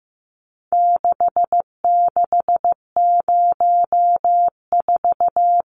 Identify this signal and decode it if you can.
Morse code, heard as a keyed tone, about 15 wpm, 700 Hz. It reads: 6604
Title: Numbers - 4-Digit 15wpm